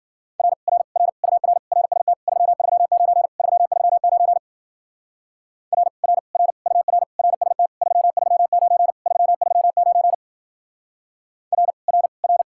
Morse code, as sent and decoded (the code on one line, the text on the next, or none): .-. / .-. / .-. / ..- .-. / .-. ... - / ...-- ...-- ----. / ...-- ...-- ----. / .-. / .-. / .-. / ..- .-. / .-. ... - / ...-- ...-- ----. / ...-- ...-- ----. / .-. / .-. / .-.
R R R UR RST 339 339 R R R UR RST 339 339 R R R